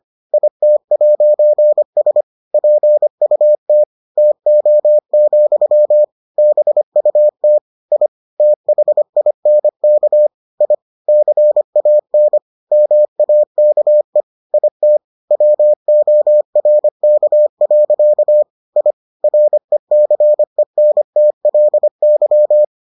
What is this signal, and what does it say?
Morse code, heard as a keyed tone, about 25 wpm, 600 Hz. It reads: IT'S PUT TO, BUT I THINK I CAN MAKE IT WORK. I RECENTLY